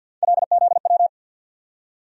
Morse code, heard as a keyed tone, about 53 wpm, 700 Hz. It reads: LZW
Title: USA Call Sign Postfixes - 3 Letters 50wpm